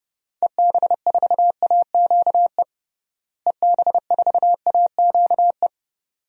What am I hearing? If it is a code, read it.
Morse code, heard as a keyed tone, about 28 wpm, 700 Hz. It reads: EB4AQE EB4AQE
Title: DXCC-100 Call Signs - 6 Characters - Repeated 2x - 30wpm